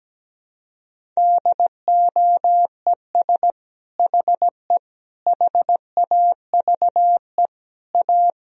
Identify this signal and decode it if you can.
Morse code, heard as a keyed tone, about 17 wpm, 700 Hz. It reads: DOES HE HAVE A
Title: Sentences from Top 1000 Words - Review Part 4 - 17wpm